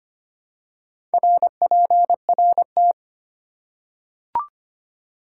Morse code, heard as a keyed tone, about 25 wpm, 700 Hz. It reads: RPRT
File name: Top 90 QSO Elements - 4x Character Spacing - 25wpm